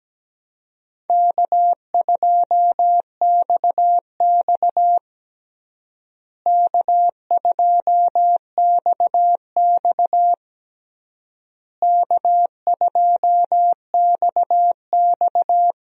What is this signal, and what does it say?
Morse code, heard as a keyed tone, about 17 wpm, 700 Hz. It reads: K2XX K2XX K2XX
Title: USA Call Signs - 4 Characters - Repeated 3x - 17wpm